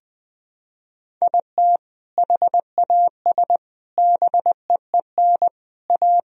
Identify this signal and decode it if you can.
Morse code, heard as a keyed tone, about 19 wpm, 700 Hz. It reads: IT HAS BEEN A
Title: Sentences from Top 1000 Words - Review Part 1 - 20wpm